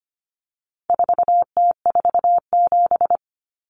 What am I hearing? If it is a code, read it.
Morse code, heard as a keyed tone, about 25 wpm, 700 Hz. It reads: 4T47
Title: Common Cut Numbers - 4-Digit 25wpm